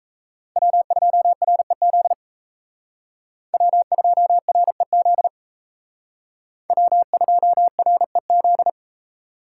Morse code, heard as a keyed tone, about 35 wpm, 700 Hz. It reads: W2REZ W2REZ W2REZ
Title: Speed Racing - USA 1x3 Call Signs 28wpm